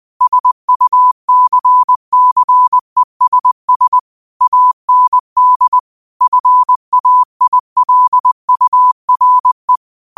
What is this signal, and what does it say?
Morse code, heard as a keyed tone, about 20 wpm, 1000 Hz. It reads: SUCCESS AND FAILURE
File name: qotdfiles / 20wpm